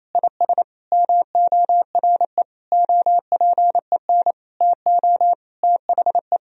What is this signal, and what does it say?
Morse code, heard as a keyed tone, about 28 wpm, 700 Hz. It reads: IS MORE OPEN TO THE
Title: Sentences from Top 1000 Words - Review Part 3 - 28wpm